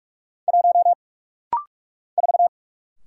Morse code, heard as a keyed tone, about 44 wpm, 700 Hz. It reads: 1 4
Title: Single Letter-Number Alt - Rapid-Fire 45wpm